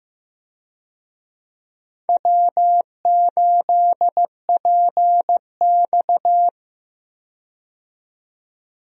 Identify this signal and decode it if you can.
Morse code, heard as a keyed tone, about 15 wpm, 700 Hz. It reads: W8PX
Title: USA Call Signs - 4 Characters - 15wpm